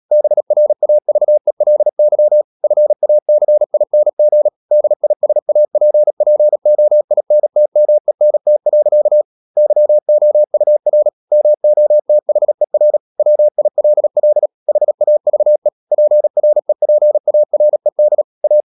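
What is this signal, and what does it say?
Morse code, heard as a keyed tone, about 37 wpm, 600 Hz. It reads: BRAVELY FACING DISAPPOINTMENT. YOUR MOTHER WILL HAVE PREPARED A
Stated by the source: Morse practice at 36 wpm, text of Ken Follett's novel Never, from CW med Gnister.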